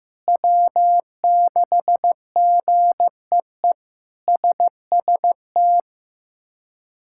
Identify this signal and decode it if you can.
Morse code, heard as a keyed tone, about 16 wpm, 700 Hz. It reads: W6GEE SST
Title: K1USN SST QSO Practice - 15wpm